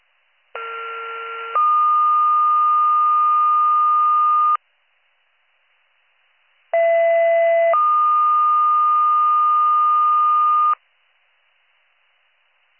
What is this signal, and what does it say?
Two-tone page, heard as a one-second tone, then a three-second tone, about 1370 then 1220 Hz, about 670 then 1220 Hz. It is heard from 0.6 to 10.7 s.